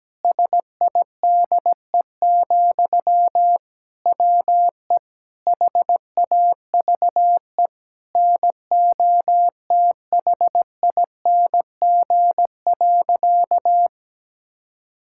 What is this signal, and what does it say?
Morse code, heard as a keyed tone, about 17 wpm, 700 Hz. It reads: SIDE, WE HAVE NOTHING.
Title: Sentences from Top 1000 Words - Review Part 5 - 17wpm